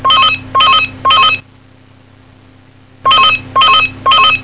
There is no code.